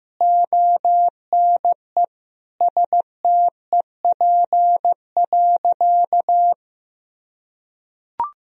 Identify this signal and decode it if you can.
Morse code, heard as a keyed tone, about 16 wpm, 700 Hz. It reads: ONE STEP.
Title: Sentences from Top 1000 Words - Review Part 2 - 15wpm